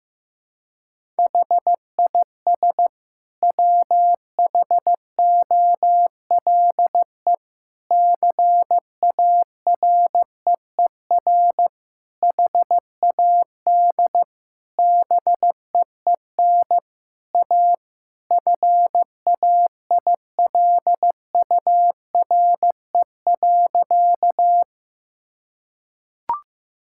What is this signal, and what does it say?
Morse code, heard as a keyed tone, about 15 wpm, 700 Hz. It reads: HIS WHOLE CAREER HAD BEEN A FAILURE.
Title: Sentences from Top 1000 Words - Review Part 6 - 15wpm